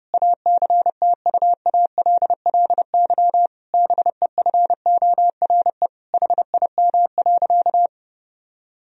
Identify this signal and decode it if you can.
Morse code, heard as a keyed tone, about 32 wpm, 700 Hz. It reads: ACTUALLY BEFORE HIM.
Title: Sentences from Top 1000 Words - Review Part 4 - 30wpm